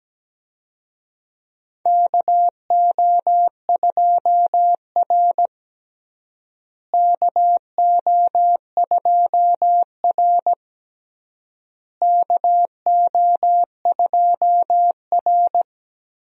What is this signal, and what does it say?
Morse code, heard as a keyed tone, about 17 wpm, 700 Hz. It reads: KO2R KO2R KO2R
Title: USA Call Signs - 4 Characters - Repeated 3x - 17wpm